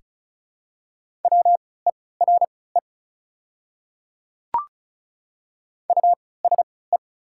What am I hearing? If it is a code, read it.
Morse code, heard as a keyed tone, about 36 wpm, spaced as for 20 wpm, 690 Hz. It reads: WERE USE
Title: 3x Character Spacing - Top 100 Words - 35wpm